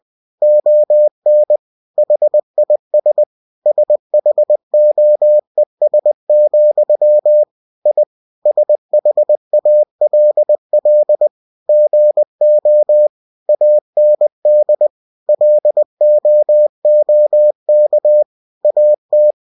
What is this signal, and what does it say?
Morse code, heard as a keyed tone, about 20 wpm, 600 Hz. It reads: ON HIS SHOES, I SHALL GO AND LOOK AT
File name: cw-books / winnie-the-pooh / winnie-20wpm-page-0043